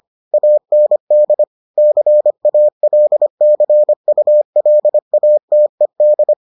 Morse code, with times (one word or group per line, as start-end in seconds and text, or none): 0.3-1.4 AND
1.8-6.3 CALCULATED